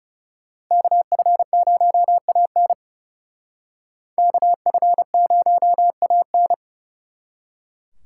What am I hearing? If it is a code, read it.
Morse code, heard as a keyed tone, about 39 wpm, 700 Hz. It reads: KF0AN KF0AN